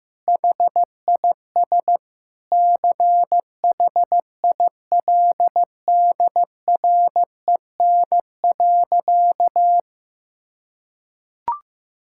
Morse code, heard as a keyed tone, about 15 wpm, 700 Hz. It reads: HIS CHILDREN.
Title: Sentences from Top 1600 Words 15wpm